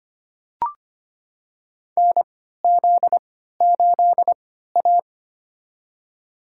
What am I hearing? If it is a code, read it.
Morse code, heard as a keyed tone, about 25 wpm, spaced as for 14 wpm, 700 Hz. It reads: NZ8A
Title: USA Call Signs - 4 Characters - 3x Character Spacing - 25wpm